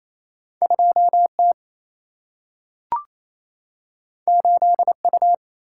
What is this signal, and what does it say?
Morse code, heard as a keyed tone, about 28 wpm, 700 Hz. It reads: 2T 8U
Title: Single-Letter Single-Number 28wpm